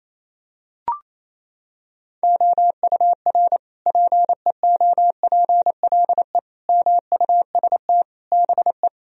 Morse code, heard as a keyed tone, about 27 wpm, 700 Hz. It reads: OUR PEOPLE MUST BE
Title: Sentences from Top 1000 Words - Review Part 4 - 28wpm